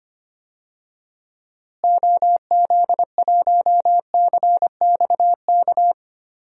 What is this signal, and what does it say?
Morse code, heard as a keyed tone, about 25 wpm, 700 Hz. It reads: OZ1CXK
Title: DXCC-100 Call Signs - 6 Characters - Repeated 3x - 25wpm